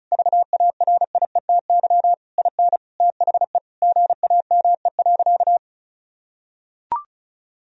Morse code, heard as a keyed tone, about 36 wpm, 700 Hz. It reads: VARIETY IN THE GAME.
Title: Sentences from Top 1000 Words - Review Part 4 - 35wpm